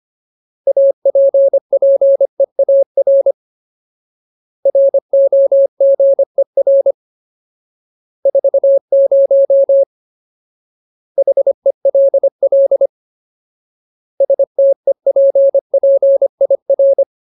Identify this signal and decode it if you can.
Morse code, heard as a keyed tone, about 25 wpm, 550 Hz. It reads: APPEAR ROGER 40 HELL STEPPIR
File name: Ultimate Word List / 25WPM Ultimate